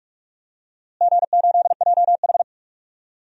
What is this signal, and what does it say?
Morse code, heard as a keyed tone, about 46 wpm, 700 Hz. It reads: G8JH
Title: DXCC-100 Call Signs - 4 Characters - Repeated 3x - 45wpm